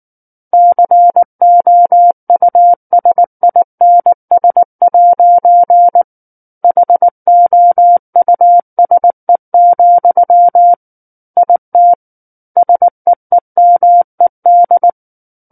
Morse code, heard as a keyed tone, about 19 wpm, 700 Hz. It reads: COUSINS' HOUSE, IT SEEMED
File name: Book One / 10-Chapter 10 - 19 WPM